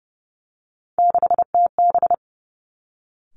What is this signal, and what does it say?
Morse code, heard as a keyed tone, about 30 wpm, 700 Hz. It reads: 6TB
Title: Three Challenging Letter-Number Combinations 30wpm